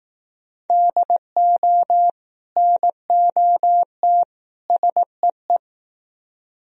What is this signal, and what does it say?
Morse code, heard as a keyed tone, about 18 wpm, 700 Hz. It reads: DO NOT SEE
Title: Speed-Racing - Sets of 3 Words 15wpm